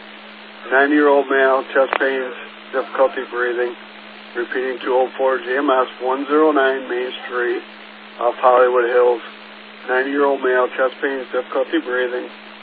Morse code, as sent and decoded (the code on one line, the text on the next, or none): none